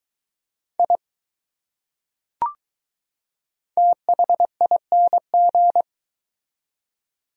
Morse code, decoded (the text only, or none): I THING